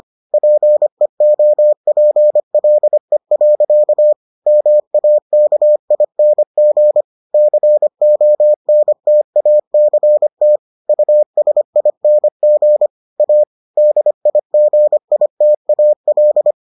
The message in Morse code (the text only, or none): PEOPLE. MAKING CONTACT USING A DIGITAL